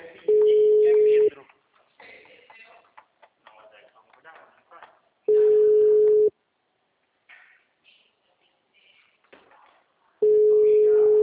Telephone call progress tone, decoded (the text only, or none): ringback tone